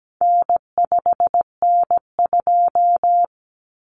Morse code, heard as a keyed tone, about 17 wpm, 700 Hz. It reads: N5N2
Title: Common Cut Numbers - 4-Digit 17wpm